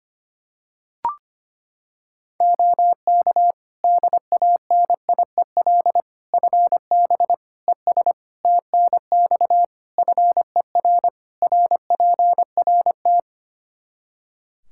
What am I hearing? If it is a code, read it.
Morse code, heard as a keyed tone, about 25 wpm, 700 Hz. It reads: OK DANIEL FB ES TNX FER RPRT
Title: Practice QSO - Exchange 2 - Info - 25wpm